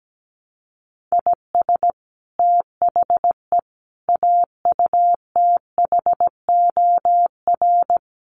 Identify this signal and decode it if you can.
Morse code, heard as a keyed tone, about 17 wpm, 700 Hz. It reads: IS THE AUTHOR
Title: Sets of 4 Words 17wpm